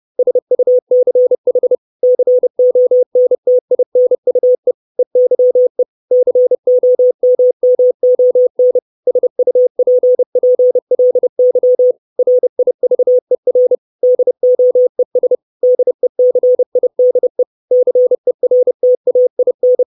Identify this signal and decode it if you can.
Morse code, heard as a keyed tone, about 30 wpm, 500 Hz. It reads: SUCH CONTINUE EYE COMMON SUPPLY RIVER DOES DECIDE CERTAIN
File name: 632 Words 30WPM